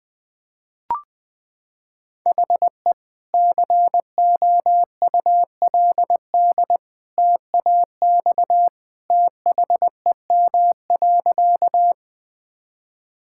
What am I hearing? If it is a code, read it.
Morse code, heard as a keyed tone, about 21 wpm, 700 Hz. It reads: HE COULD TAX THEM.
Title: Sentences from Top 1000 Words - Review Part 5 - 20wpm